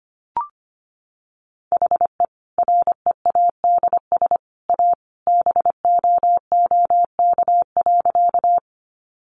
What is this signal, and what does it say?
Morse code, heard as a keyed tone, about 25 wpm, 700 Hz. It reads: HE READS A BOOK.
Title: Additional Easy Phrases 25wpm